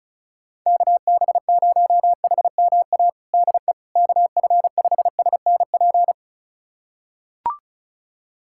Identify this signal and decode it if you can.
Morse code, heard as a keyed tone, about 36 wpm, 700 Hz. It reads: KB0HMA DE KF5SNP
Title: USA Callsign DE Callsign - 6 Characters - 35wpm